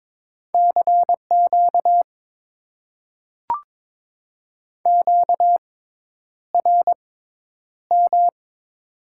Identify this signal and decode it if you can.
Morse code, heard as a keyed tone, about 22 wpm, spaced as for 10 wpm, 700 Hz. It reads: CQ QRM
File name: Top 90 QSO Elements - 6x Character Spacing - 22wpm